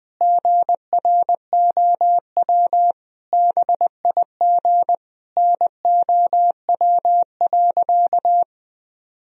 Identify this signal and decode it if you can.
Morse code, heard as a keyed tone, about 20 wpm, 700 Hz. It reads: GROW BIG NOW.
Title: Sentences from Top 1000 Words - Review Part 1 - 20wpm